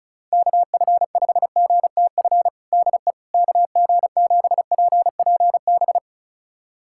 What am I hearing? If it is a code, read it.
Morse code, heard as a keyed tone, about 36 wpm, 700 Hz. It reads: KF5GTF DE KG7PPB
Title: USA Callsign DE Callsign 35wpm